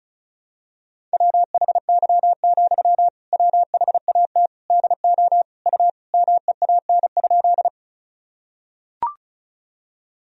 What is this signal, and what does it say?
Morse code, heard as a keyed tone, about 36 wpm, 700 Hz. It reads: WHY, WHAT DO U MEAN?
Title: Sentences from Top 200 Words and Abbreviations 35wpm